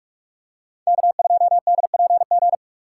Morse code, heard as a keyed tone, about 41 wpm, 700 Hz. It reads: K2DPG